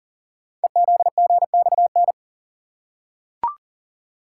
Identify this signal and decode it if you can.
Morse code, heard as a keyed tone, about 38 wpm, 700 Hz. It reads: EZGXN
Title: Code Groups - 5 Letter:Numbers - 40wpm